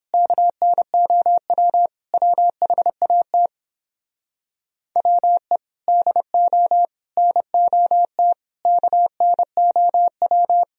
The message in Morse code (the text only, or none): KNOW WHAT WE DO NOT KNOW